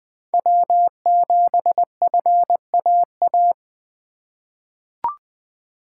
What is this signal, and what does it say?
Morse code, heard as a keyed tone, about 20 wpm, 700 Hz. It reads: W7FAA
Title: USA Call Signs - 4x Character Spacing - 20wpm